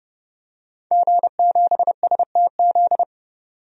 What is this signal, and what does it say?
Morse code, heard as a keyed tone, about 30 wpm, 700 Hz. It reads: G7STZ
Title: DXCC-100 Call Signs - 5 Characters - 30wpm